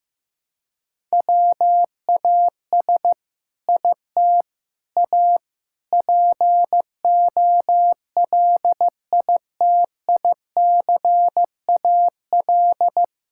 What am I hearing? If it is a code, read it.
Morse code, heard as a keyed tone, about 15 wpm, 700 Hz. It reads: WAS IT A POLITICAL
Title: Sentences from Top 1000 Words - Review Part 5 - 15wpm